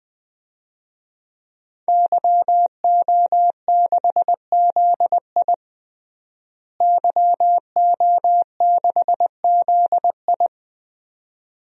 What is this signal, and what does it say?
Morse code, heard as a keyed tone, about 20 wpm, 700 Hz. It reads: YO6ZI YO6ZI
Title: DXCC-100 Call Signs - 5 Characters - Repeated 2x - 20wpm